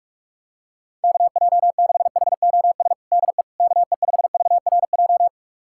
Morse code, heard as a keyed tone, about 46 wpm, 700 Hz. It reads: KJ6HOS DE KE5VRJ
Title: USA Callsign DE Callsign - 6 Characters - 45wpm